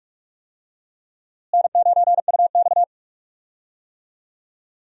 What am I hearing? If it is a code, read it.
Morse code, heard as a keyed tone, about 46 wpm, 700 Hz. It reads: N9UX
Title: USA Call Signs - 4 Characters - Repeated 3x - 45wpm